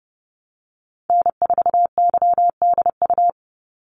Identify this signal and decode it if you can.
Morse code, heard as a keyed tone, about 30 wpm, 700 Hz. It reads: N4YDU
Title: Call Signs - 30wpm